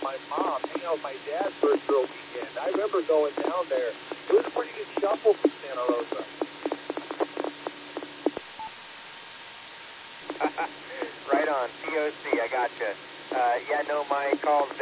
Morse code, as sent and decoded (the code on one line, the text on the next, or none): none